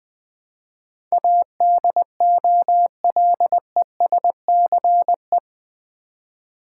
Morse code, heard as a keyed tone, about 20 wpm, 700 Hz. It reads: ADOLESCE
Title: Build A Word - 4 Letters At A Time - 10-Letter Words - 20wpm